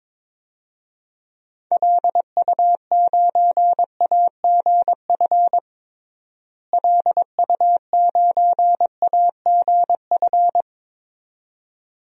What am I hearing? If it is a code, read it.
Morse code, heard as a keyed tone, about 22 wpm, 700 Hz. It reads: LU9AGF LU9AGF
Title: DXCC-100 Call Signs - 6 Characters - Repeated 2x - 22wpm